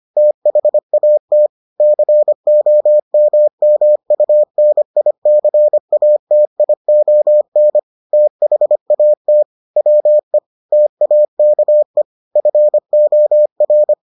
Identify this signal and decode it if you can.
Morse code, heard as a keyed tone, about 25 wpm, 600 Hz. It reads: THAT COMMUNICATION THAT WE TAKE FOR